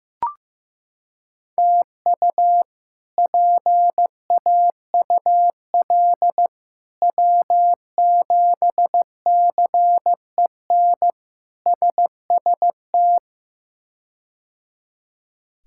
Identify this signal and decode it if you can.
Morse code, heard as a keyed tone, about 16 wpm, 700 Hz. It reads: TU PAUL W7CEN SST
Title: K1USN SST QSO Practice - 15wpm